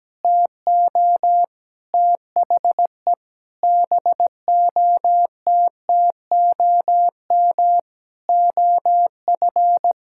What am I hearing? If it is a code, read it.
Morse code, heard as a keyed tone, about 17 wpm, 700 Hz. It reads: TO THE BOTTOM OF
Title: Sentences from Top 1000 Words - Review Part 6 - 17wpm